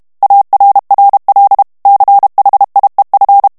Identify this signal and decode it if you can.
Morse code, heard as a keyed tone, about 32 wpm, 800 Hz. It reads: ARRL CHIEF
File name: ANGLAIS texte clair 3 vitesse 30 mots minute